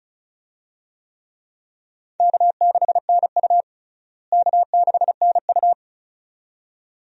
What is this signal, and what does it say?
Morse code, heard as a keyed tone, about 36 wpm, 700 Hz. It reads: K6NU K6NU
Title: USA Call Signs - 4 Characters - Repeated 2x - 35wpm